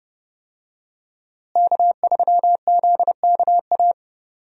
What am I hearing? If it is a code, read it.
Morse code, heard as a keyed tone, about 32 wpm, 700 Hz. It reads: K3ZKA
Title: USA Call Signs - 5 Characters - Repeated 2x - 30wpm